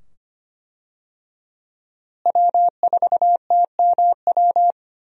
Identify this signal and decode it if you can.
Morse code, heard as a keyed tone, about 25 wpm, 700 Hz. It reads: W4TMW
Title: USA Call Signs - 5 Characters - Repeated 2x - 25wpm